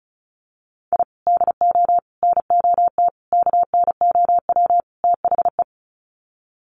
Morse code, heard as a keyed tone, about 34 wpm, 700 Hz. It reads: I DO NOT KNOW THE
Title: Sets of 5 Words 35wpm